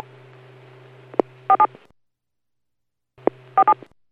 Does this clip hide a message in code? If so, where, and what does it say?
Yes: DTMF from 1.5 to 3.7 s, heard as telephone touch tones, tones of about 70 ms. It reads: 47 47